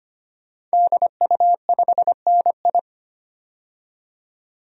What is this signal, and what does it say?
Morse code, heard as a keyed tone, about 25 wpm, 700 Hz. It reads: DU5NI
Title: DXCC-100 Call Signs - 5 Characters - 25wpm